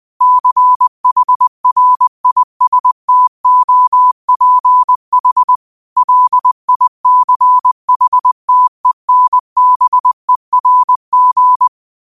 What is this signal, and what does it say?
Morse code, heard as a keyed tone, about 20 wpm, 1000 Hz. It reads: CHRISTOPH LICHTENBERG
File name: qotdfiles / 20wpm